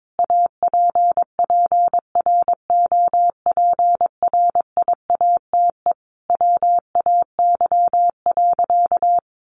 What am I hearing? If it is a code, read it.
Morse code, heard as a keyed tone, about 22 wpm, 700 Hz. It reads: APPROPRIATE WAY.
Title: Sentences from Top 900 Words 22wpm